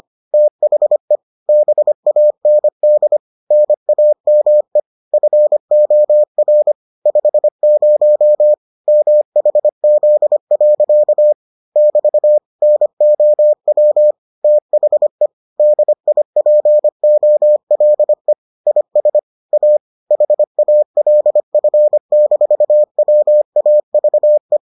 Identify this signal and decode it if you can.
Morse code, heard as a keyed tone, about 25 wpm, 600 Hz. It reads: THE BAND NAME FOR 50 MHZ. <BT> NOW THE DIPOLE IS A HALF-WAVE